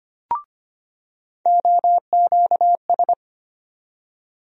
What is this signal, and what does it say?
Morse code, heard as a keyed tone, about 25 wpm, 700 Hz. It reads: OQS
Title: USA Call Sign Postfixes - 3 Letters 25wpm